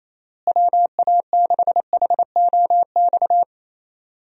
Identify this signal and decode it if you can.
Morse code, heard as a keyed tone, about 28 wpm, 700 Hz. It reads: WA6HOX
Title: USA Call Signs - 6 Characters - Repeated 2x - 28wpm